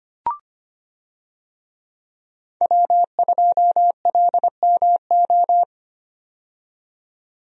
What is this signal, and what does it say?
Morse code, heard as a keyed tone, about 25 wpm, 700 Hz. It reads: W2LMO
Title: USA Call Signs - 5 Characters - 25wpm